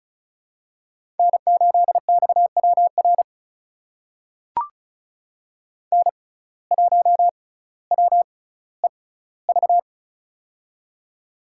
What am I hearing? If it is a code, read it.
Morse code, heard as a keyed tone, about 36 wpm, spaced as for 18 wpm, 700 Hz. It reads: N8XWR N1WEV
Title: USA Call Signs - 5 Characters - 6x Character Spacing - 35wpm